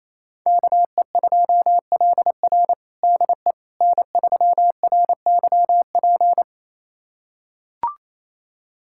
Morse code, heard as a keyed tone, about 28 wpm, 700 Hz. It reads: KE2LR DE N3RYP
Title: USA Callsign DE Callsign - 5 Characters - 28wpm